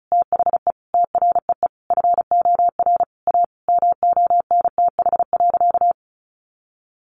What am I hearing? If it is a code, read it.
Morse code, heard as a keyed tone, about 34 wpm, 700 Hz. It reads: THE TREE FOR A MONTH.
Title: Easy Phrases 35wpm